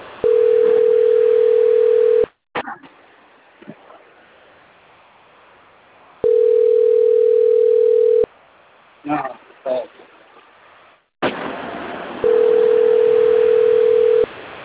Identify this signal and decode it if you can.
Ringback tone (North America), 440 + 480 Hz, with speech.
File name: rg-1600-unknown- - - .33505